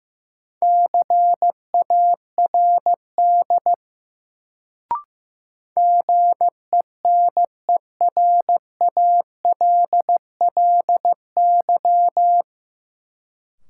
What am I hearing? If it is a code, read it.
Morse code, heard as a keyed tone, about 15 wpm, 700 Hz. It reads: CARD GENERALLY
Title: Sentences from Top 600 Words - New Word Drill - 15wpm